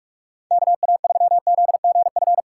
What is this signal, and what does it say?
Morse code, heard as a keyed tone, about 41 wpm, 700 Hz. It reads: KA3ZGF
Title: USA Callsign DE Callsign 45wpm